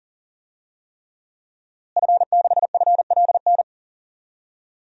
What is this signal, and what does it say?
Morse code, heard as a keyed tone, about 40 wpm, 700 Hz. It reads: F6FLN
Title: DXCC-100 Call Signs - 5 Characters - Repeated 3x - 40wpm